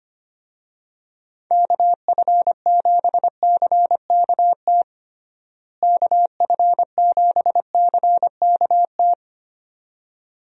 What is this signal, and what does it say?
Morse code, heard as a keyed tone, about 25 wpm, 700 Hz. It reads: KF7CKT KF7CKT
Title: USA Call Signs - 6 Characters - Repeated 2x - 25wpm